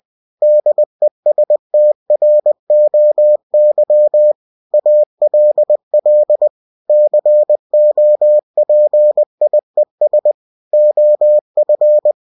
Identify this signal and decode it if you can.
Morse code, heard as a keyed tone, about 20 wpm, 600 Hz. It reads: DESTROY ALL COPIES OF
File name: cw-books / winnie-the-pooh / winnie-20wpm-page-0113